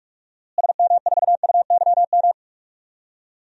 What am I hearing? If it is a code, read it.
Morse code, heard as a keyed tone, about 46 wpm, 690 Hz. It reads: SM4UYM
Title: DXCC-100 Call Signs - 6 Characters - Repeated 2x - 45wpm